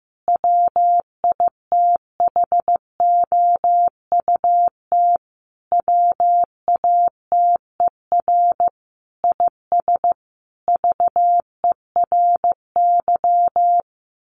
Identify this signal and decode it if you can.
Morse code, heard as a keyed tone, about 15 wpm, 700 Hz. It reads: WITHOUT WATER IS VERY
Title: Sentences from Top 1500 Words 15wpm